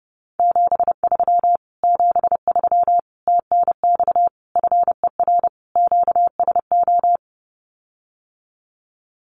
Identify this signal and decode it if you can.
Morse code, heard as a keyed tone, about 30 wpm, 700 Hz. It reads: 73 73 TNX FER QSO
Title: Common QSO Phrases Encore 30wpm